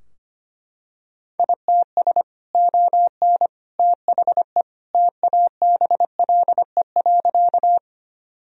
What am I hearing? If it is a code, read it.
Morse code, heard as a keyed tone, about 25 wpm, 700 Hz. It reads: ITS ON THE TABLE.
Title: Sentences from Top 1000 Words - Review Part 2 - 25wpm